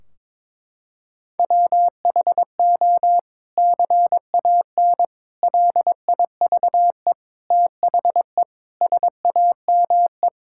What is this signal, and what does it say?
Morse code, heard as a keyed tone, about 21 wpm, 700 Hz. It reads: WHO CAN LIVE THE SAME
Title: Sentences from Top 1600 Words 22wpm